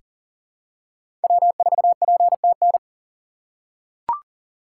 Morse code, heard as a keyed tone, about 40 wpm, 700 Hz. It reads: W4PTN
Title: USA Call Signs - 5 Characters - 3x Character Spacing - 40wpm